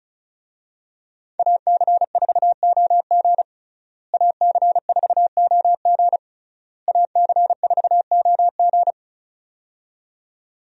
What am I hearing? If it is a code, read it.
Morse code, heard as a keyed tone, about 36 wpm, 700 Hz. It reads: AC4OG AC4OG AC4OG
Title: USA Call Signs - 5 Characters - Repeated 3x - 35wpm